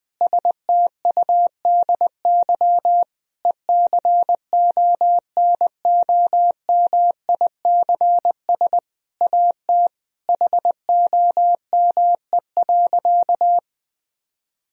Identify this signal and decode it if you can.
Morse code, heard as a keyed tone, about 20 wpm, 700 Hz. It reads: STUDY ECONOMICS AT HOME.
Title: Sentences from Top 1000 Words - Review Part 3 - 20wpm